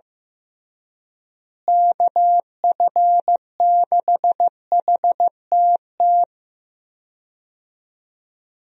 Morse code, heard as a keyed tone, about 15 wpm, 700 Hz. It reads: KF6HTT
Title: USA Call Signs - 6 Characters - Repeated 3x - 15wpm